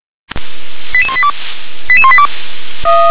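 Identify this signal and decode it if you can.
Five-tone selective call (ZVEI), heard as quick runs of five tones, tones of about 70 ms. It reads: 88182 88182
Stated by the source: ZVEI five-tone selective call 88225 received via SDR.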